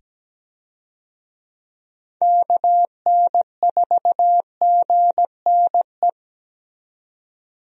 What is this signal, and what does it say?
Morse code, heard as a keyed tone, about 17 wpm, 700 Hz. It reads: KN4GNE
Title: USA Call Signs - 6 Characters - 17wpm